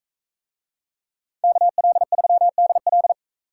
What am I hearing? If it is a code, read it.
Morse code, heard as a keyed tone, about 44 wpm, 700 Hz. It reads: KL3DL